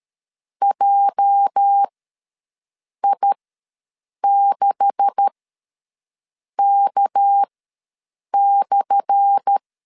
Morse code, heard as a keyed tone, about 13 wpm, spaced as for 6 wpm, 800 Hz. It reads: JI6K/
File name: LEZIONI / 10 SIGNS / 10A SLASH ws 12wpm 3443